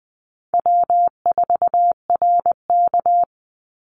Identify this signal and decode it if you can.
Morse code, heard as a keyed tone, about 20 wpm, 700 Hz. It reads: W4RK
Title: Easy Call Signs Repeated 3x - 20wpm